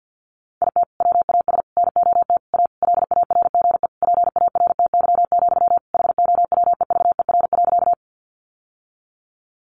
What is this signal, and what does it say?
Morse code, heard as a keyed tone, about 50 wpm, 700 Hz. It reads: IT WAS NOT A LARGE PARTY, HOWEVER.
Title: Sentences from Top 1400 Words 50wpm